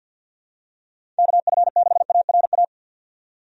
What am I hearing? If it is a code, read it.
Morse code, heard as a keyed tone, about 51 wpm, 700 Hz. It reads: KF6ARA
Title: USA Call Signs - 6 Characters - 50wpm